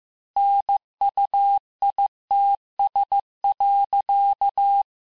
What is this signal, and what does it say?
Morse code, heard as a keyed tone, about 15 wpm, 800 Hz. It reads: NUITS.
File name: cours-du-lundi files / cours-1585-texte-FIN-de-QSO-vitesse-14mpm